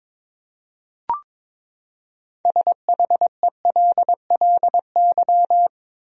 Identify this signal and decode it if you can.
Morse code, heard as a keyed tone, about 22 wpm, 700 Hz. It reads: SHELLY
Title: USA Common Names - 22wpm